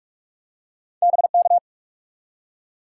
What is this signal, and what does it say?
Morse code, heard as a keyed tone, about 43 wpm, 700 Hz. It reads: BK
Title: Top 90 QSO Elements 45wpm